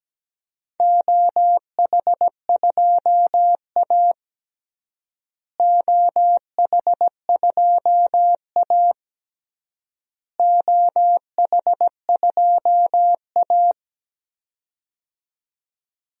Morse code, heard as a keyed tone, about 17 wpm, 700 Hz. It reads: OH2A OH2A OH2A
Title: DXCC-100 Call Signs - 4 Characters - Repeated 3x - 17wpm